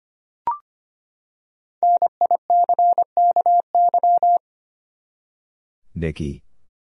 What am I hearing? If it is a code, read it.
Morse code, heard as a keyed tone, about 25 wpm, 700 Hz. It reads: NICKY
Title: USA Common Names - 25wpm